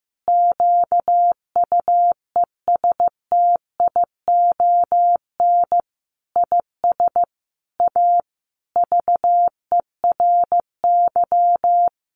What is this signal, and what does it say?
Morse code, heard as a keyed tone, about 15 wpm, 700 Hz. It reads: QUESTION IS A VERY
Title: Sentences from Top 1200 Words 15wpm